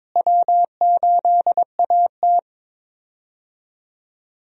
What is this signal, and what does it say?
Morse code, heard as a keyed tone, about 22 wpm, 700 Hz. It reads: W8AT
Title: USA Call Signs - 4 Characters - Repeated 2x - 22wpm